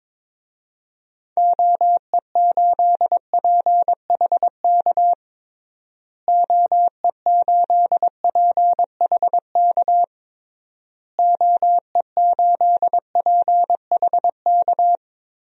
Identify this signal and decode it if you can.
Morse code, heard as a keyed tone, about 22 wpm, 700 Hz. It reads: OE8PHK OE8PHK OE8PHK
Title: DXCC-100 Call Signs - 6 Characters - Repeated 3x - 22wpm